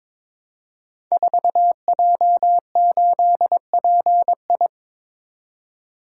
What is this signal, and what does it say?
Morse code, heard as a keyed tone, about 23 wpm, 700 Hz. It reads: 4J8PI
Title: Code Groups - 5 Letter:Numbers - 22wpm